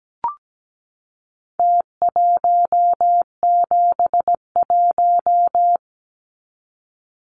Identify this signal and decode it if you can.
Morse code, heard as a keyed tone, about 17 wpm, 700 Hz. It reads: T171
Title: Numbers - Contest Style - 4-Digit 17wpm